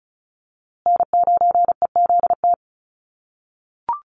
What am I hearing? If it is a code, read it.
Morse code, heard as a keyed tone, about 34 wpm, 700 Hz. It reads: N9EZT
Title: USA Call Signs Repeated 3x 35wpm